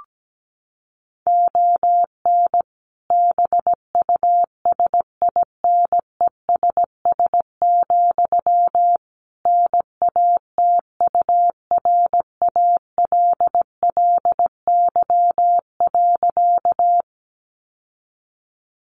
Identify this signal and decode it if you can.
Morse code, heard as a keyed tone, about 17 wpm, 700 Hz. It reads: ON BUSINESS, NATURALLY.
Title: Sentences from Top 1300 Words 17wpm